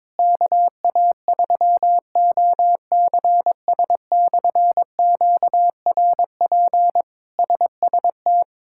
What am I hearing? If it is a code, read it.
Morse code, heard as a keyed tone, about 22 wpm, 700 Hz. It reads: KA3OCS/QRP SST
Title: K1USN SST QSO Practice - 22wpm